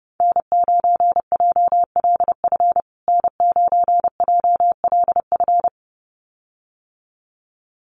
Morse code, heard as a keyed tone, about 28 wpm, 700 Hz. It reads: N9JLF N9JLF
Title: USA Call Signs Repeated 2x 30wpm